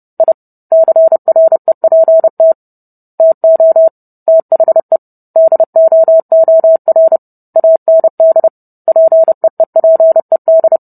Morse code, heard as a keyed tone, about 30 wpm, 650 Hz. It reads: I CREPT TO THE DOOR AND PEEPED